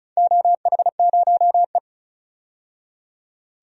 Morse code, heard as a keyed tone, about 36 wpm, 700 Hz. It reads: OH0E
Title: DXCC-100 Call Signs - 4 Characters - 35wpm